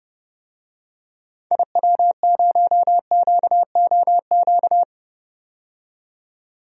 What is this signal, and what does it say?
Morse code, heard as a keyed tone, about 30 wpm, 700 Hz. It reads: IW0QOQ
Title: DXCC-100 Call Signs - 6 Characters - 30wpm